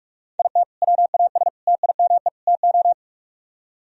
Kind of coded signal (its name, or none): Morse code